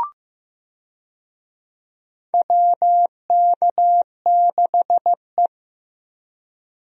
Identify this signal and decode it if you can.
Morse code, heard as a keyed tone, about 15 wpm, 700 Hz. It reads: WK6E
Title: USA Call Signs - 4 Characters - Repeated 3x - 15wpm